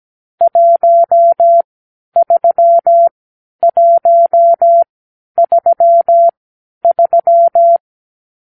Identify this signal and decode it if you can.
Morse code, heard as a keyed tone, about 17 wpm, spaced as for 10 wpm, 680 Hz. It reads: 13133
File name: arrl / Review (1-5)